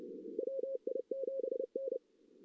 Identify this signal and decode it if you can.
Morse code, heard as a keyed tone, about 30 wpm, 530 Hz. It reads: WI7N
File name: 2016 JAN CW NAQP / 2016 Jan naqp k5ot - 0-4